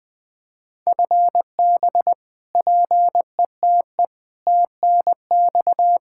Morse code, heard as a keyed tone, about 21 wpm, 700 Hz. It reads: FB PETE TNX
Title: Practice QSO - Exchange 3 - More Info - 20wpm